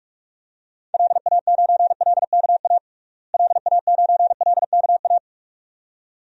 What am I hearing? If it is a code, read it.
Morse code, heard as a keyed tone, about 42 wpm, 700 Hz. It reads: LA9LKA LA9LKA
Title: DXCC-100 Call Signs - 6 Characters - Repeated 2x - 45wpm